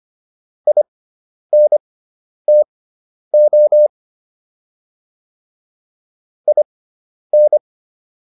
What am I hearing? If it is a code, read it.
Morse code, heard as a keyed tone, about 25 wpm, spaced as for 10 wpm, 600 Hz. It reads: INTO IN